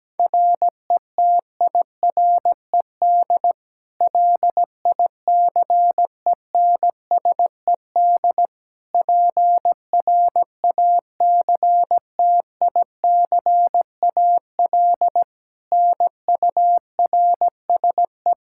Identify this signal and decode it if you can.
Morse code, heard as a keyed tone, about 17 wpm, 700 Hz. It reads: RETIRED LICENSED PRACTICAL NURSE
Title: Practice QSO - Exchange 3 - More Info - 17wpm